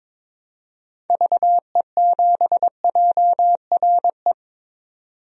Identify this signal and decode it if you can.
Morse code, heard as a keyed tone, about 22 wpm, 700 Hz. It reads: VE7JRE
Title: DXCC-100 Call Signs - 6 Characters - 22wpm